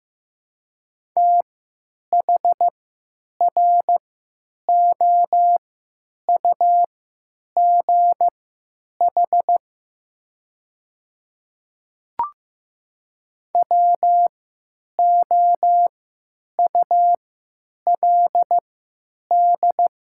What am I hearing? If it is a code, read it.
Morse code, heard as a keyed tone, about 15 wpm, spaced as for 9 wpm, 700 Hz. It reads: THROUGH WOULD
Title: 3x Character Spacing - Top 200 Words - 15wpm